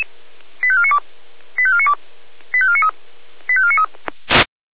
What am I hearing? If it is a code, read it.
Five-tone selective call (ZVEI), heard as quick runs of five tones, tones of about 70 ms. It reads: 85492 85492 85493 85493
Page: /496/SDR/ZVEI/